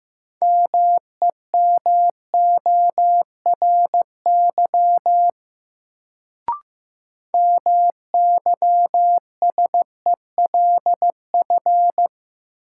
Morse code, heard as a keyed tone, about 15 wpm, 700 Hz. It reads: MEMORY MYSELF
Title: Sentences from Top 1300 Words - New Word Drill - 15wpm